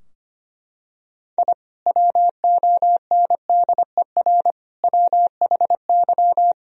Morse code, heard as a keyed tone, about 25 wpm, 700 Hz. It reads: I WONDER WHY
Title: Sentences from Top 1000 Words - Review Part 3 - 25wpm